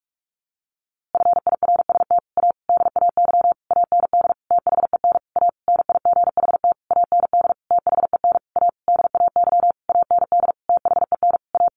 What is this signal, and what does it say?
Morse code, heard as a keyed tone, about 45 wpm, 700 Hz. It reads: FIRST A DAY AND THEN A NIGHT AND THEN A DAY AND THEN A
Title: A Sound Of Thunder 45wpm